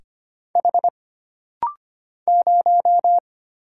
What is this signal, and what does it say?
Morse code, heard as a keyed tone, about 25 wpm, 700 Hz. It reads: H 0
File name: Single Character (Letter:Number:Punctuation:Prosign) - Rapid-Fire - 25wpm